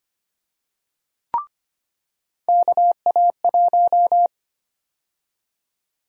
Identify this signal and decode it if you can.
Morse code, heard as a keyed tone, about 25 wpm, 700 Hz. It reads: KA1
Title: USA Call Sign Prefixes - 25wpm